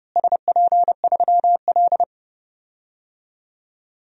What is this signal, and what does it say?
Morse code, heard as a keyed tone, about 30 wpm, 700 Hz. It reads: SP3L
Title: DXCC-100 Call Signs - 4 Characters - Repeated 2x - 30wpm